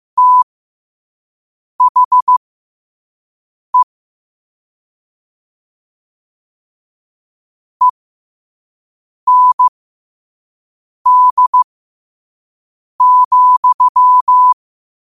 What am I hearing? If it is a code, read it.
Morse code, heard as a keyed tone, about 15 wpm, spaced as for 5 wpm, 1000 Hz. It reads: THE END,
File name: qotdfiles / 5wpm